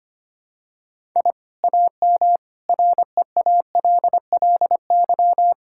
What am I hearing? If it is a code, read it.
Morse code, heard as a keyed tone, about 25 wpm, 700 Hz. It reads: I AM REALLY
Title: Sentences from Top 1000 Words - Review Part 6 - 25wpm